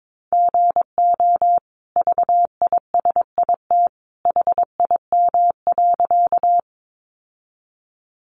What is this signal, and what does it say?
Morse code, heard as a keyed tone, about 22 wpm, 700 Hz. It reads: GO VISIT HIM.
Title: Sentences from Top 900 Words 22wpm